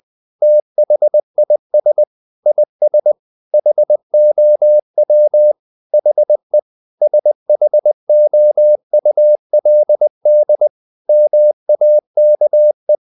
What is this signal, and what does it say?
Morse code, heard as a keyed tone, about 20 wpm, 600 Hz. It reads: THIS IS HOW HE SHOULD MAKE
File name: the old man and the sea / the old man and the sea part 1 20wpm